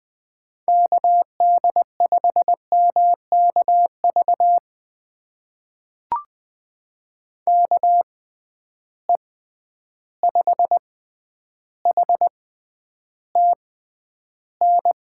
Morse code, heard as a keyed tone, about 21 wpm, spaced as for 10 wpm, 700 Hz. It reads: KD5MKV KE5HTN
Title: USA Call Signs - 6 Characters - 6x Character Spacing - 20wpm